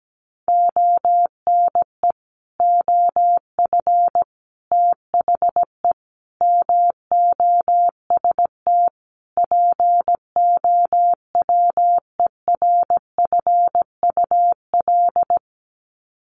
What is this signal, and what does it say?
Morse code, heard as a keyed tone, about 17 wpm, 700 Hz. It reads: ONE OF THE MOST POWERFUL
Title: Sets of 5 Words Encore 17wpm